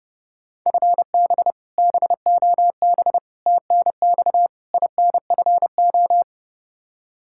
Morse code, heard as a keyed tone, about 30 wpm, 700 Hz. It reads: FB BOB TNX INFO
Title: Practice QSO - Exchange 3 - More Info - 30wpm